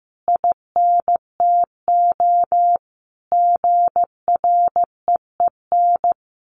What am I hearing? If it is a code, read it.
Morse code, heard as a keyed tone, about 15 wpm, 700 Hz. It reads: INTO GREEN
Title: A Sound Of Thunder 15wpm